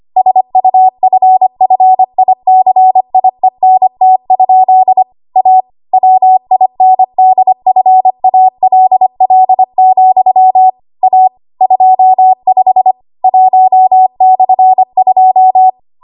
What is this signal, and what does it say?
Morse code, heard as a keyed tone, about 25 wpm, 750 Hz. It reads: SUFFICIENT? A WINDFALL, A 25 1/2
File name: Archive / 25 WPM / 25WPM